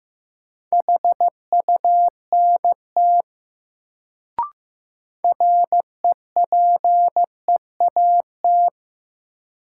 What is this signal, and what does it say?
Morse code, heard as a keyed tone, about 15 wpm, 700 Hz. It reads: HUNT REPEAT
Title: Sentences from Top 1600 Words - New Word Drill - 15wpm